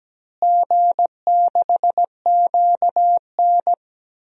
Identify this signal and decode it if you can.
Morse code, heard as a keyed tone, about 17 wpm, 700 Hz. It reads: G6QN
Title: DXCC-100 Call Signs - 4 Characters - Repeated 3x - 17wpm